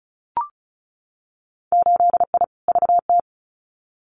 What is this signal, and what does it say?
Morse code, heard as a keyed tone, about 36 wpm, 700 Hz. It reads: 8I VT
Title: Winter Field Day Exchanges Encore 35wpm